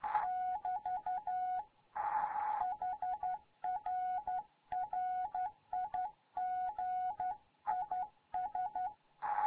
Morse code, heard as a keyed tone, about 11 wpm, 740 Hz. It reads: <BT> HR RIG IS
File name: Morze-HA2MN